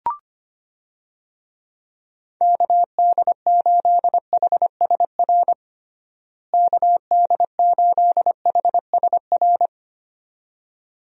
Morse code, heard as a keyed tone, about 25 wpm, 700 Hz. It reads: KD8HSR KD8HSR
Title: USA Call Signs - 6 Characters - Repeated 2x - 25wpm